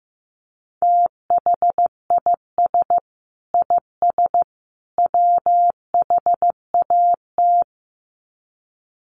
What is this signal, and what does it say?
Morse code, heard as a keyed tone, about 15 wpm, 700 Hz. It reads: THIS IS WHAT
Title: Sets of 3 Words 15wpm